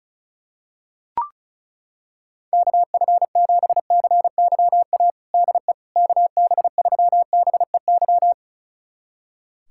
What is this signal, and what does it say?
Morse code, heard as a keyed tone, about 36 wpm, 700 Hz. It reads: KF7CYA DE KB3BEY
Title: USA Callsign DE Callsign - 6 Characters - 35wpm